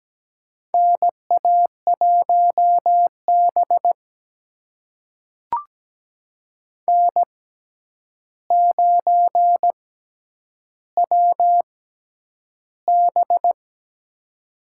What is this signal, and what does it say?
Morse code, heard as a keyed tone, about 17 wpm, spaced as for 9 wpm, 700 Hz. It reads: NA1B N9WB
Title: USA Call Signs - 4 Characters - 6x Character Spacing - 17wpm